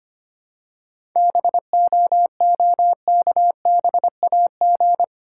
Morse code, heard as a keyed tone, about 25 wpm, 690 Hz. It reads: BOOKBAG
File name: practice / 25wpm / Easy Words with Hints - 25wpm